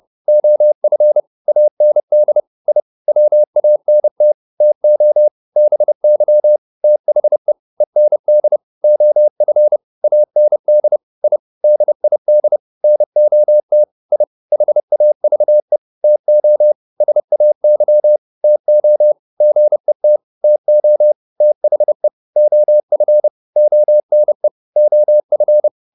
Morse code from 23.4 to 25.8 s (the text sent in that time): ONE OF